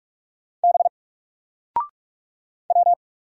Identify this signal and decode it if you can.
Morse code, heard as a keyed tone, about 46 wpm, 700 Hz. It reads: B W
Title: Single Character (Letter:Number:Punctuation:Prosign) Alt 45wpm